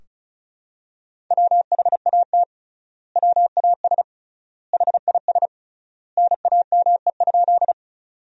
Morse code, heard as a keyed tone, about 36 wpm, 700 Hz. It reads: WHAT WAS HIS NAME?
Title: 3x Word-Spacing - Sentences from Top 200 Words - 35wpm